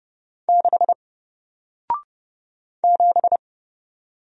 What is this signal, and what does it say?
Morse code, heard as a keyed tone, about 30 wpm, 700 Hz. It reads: 6 7
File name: Single-Letter-Number-30wpm